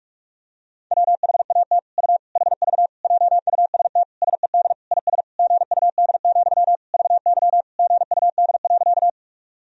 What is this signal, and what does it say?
Morse code, heard as a keyed tone, about 46 wpm, 700 Hz. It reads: WHAT U HV JUST SED IS GUD, VY GUD.
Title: Sentences from Top 200 Words and Abbreviations 45wpm